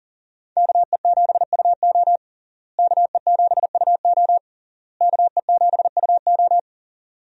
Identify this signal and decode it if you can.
Morse code, heard as a keyed tone, about 40 wpm, 700 Hz. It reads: KE7UO KE7UO KE7UO
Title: USA Call Signs - 5 Characters - Repeated 3x - 40wpm